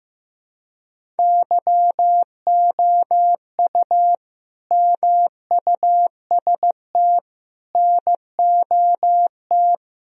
Morse code, heard as a keyed tone, about 15 wpm, 700 Hz. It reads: YOU MUST NOT
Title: Sentences from Top 1000 Words - Review Part 1 - 15wpm